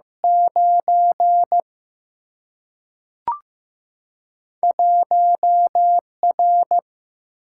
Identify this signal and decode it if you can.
Morse code, heard as a keyed tone, about 15 wpm, 700 Hz. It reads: 9 1 R
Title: Single-Letter Single-Number 15wpm